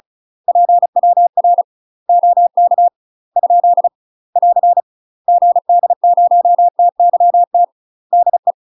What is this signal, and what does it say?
Morse code, heard as a keyed tone, about 34 wpm, 700 Hz. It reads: PWR OK ? <AR> GD0TYT DE
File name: UK MASTER QSOs TRAINING/35WPM